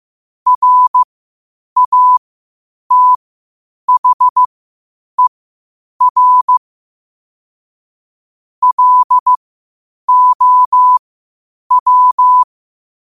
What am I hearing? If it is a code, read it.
Morse code, heard as a keyed tone, about 15 wpm, spaced as for 8 wpm, 1000 Hz. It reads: RATHER LOW
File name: qotdfiles / 8wpm